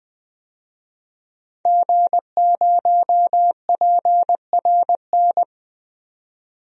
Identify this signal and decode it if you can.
Morse code, heard as a keyed tone, about 21 wpm, 700 Hz. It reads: G0PRN
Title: DXCC-100 Call Signs - 5 Characters - 20wpm